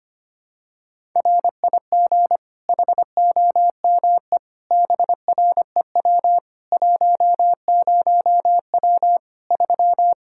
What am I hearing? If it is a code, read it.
Morse code, heard as a keyed tone, about 25 wpm, 700 Hz. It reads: RIG HOME BREW 10W 3
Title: Practice QSO - Exchange 2 - Info - 25wpm